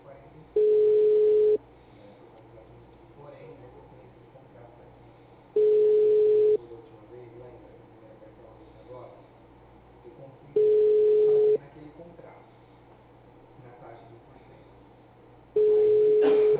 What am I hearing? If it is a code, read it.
Ringback tone (Europe, 425 Hz), with speech.
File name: out- -2018- - - .8187